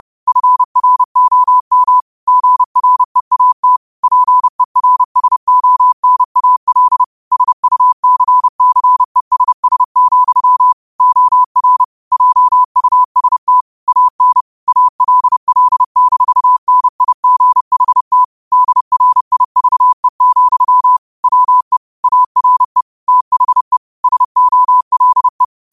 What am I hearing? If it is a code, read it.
Morse code, heard as a keyed tone, about 30 wpm, 1000 Hz. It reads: FROM GREAT PERSONAL SUCCESS, OR JUST AN ALL-NIGHT DRIVE, WE ARE THE SOLE